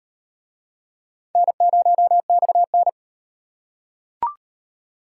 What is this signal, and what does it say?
Morse code, heard as a keyed tone, about 38 wpm, 700 Hz. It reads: N0XN